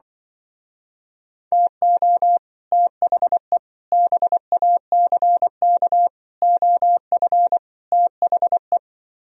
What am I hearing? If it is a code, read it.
Morse code, heard as a keyed tone, about 24 wpm, 700 Hz. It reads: TO THE BACK OF THE
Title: Speed-Racing - Sets of 5 Words 20wpm